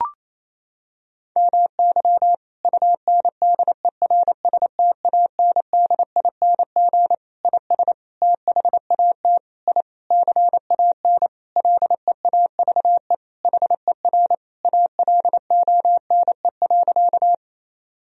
Morse code, heard as a keyed tone, about 28 wpm, 700 Hz. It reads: MY UNDERSTANDING IS THAT I CAN LEAVE HER ALONE.
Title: Sentences from Top 1000 Words - Review Part 3 - 28wpm